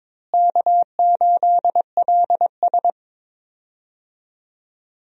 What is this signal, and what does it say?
Morse code, heard as a keyed tone, about 22 wpm, 700 Hz. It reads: K8LS
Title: USA Call Signs - 4 Characters - Repeated 2x - 22wpm